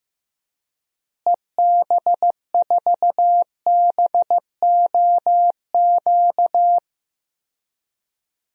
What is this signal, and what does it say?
Morse code, heard as a keyed tone, about 15 wpm, 700 Hz. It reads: EB4BOQ
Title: DXCC-100 Call Signs - 6 Characters - Repeated 3x - 15wpm